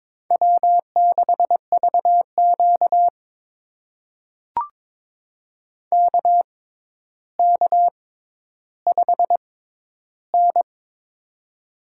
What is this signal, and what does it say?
Morse code, heard as a keyed tone, about 22 wpm, spaced as for 11 wpm, 700 Hz. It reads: W6VQ KK5N
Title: USA Call Signs - 4 Characters - 6x Character Spacing - 22wpm